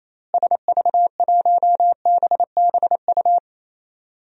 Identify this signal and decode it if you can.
Morse code, heard as a keyed tone, about 28 wpm, 700 Hz. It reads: SV1BBU
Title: DXCC-100 Call Signs - 6 Characters - Repeated 2x - 28wpm